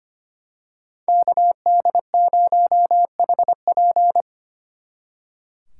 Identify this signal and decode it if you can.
Morse code, heard as a keyed tone, about 28 wpm, 700 Hz. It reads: KD0HP
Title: Speed Racing - USA Call Signs 25wpm